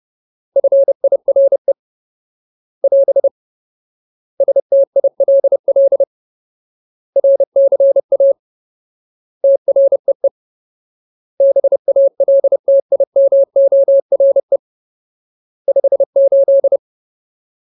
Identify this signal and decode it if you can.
Morse code, heard as a keyed tone, about 30 wpm, 550 Hz. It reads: FIRE <AS> STILL RCA TREE BALTIMORE 58
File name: Ultimate Word List / 30WPM Ultimate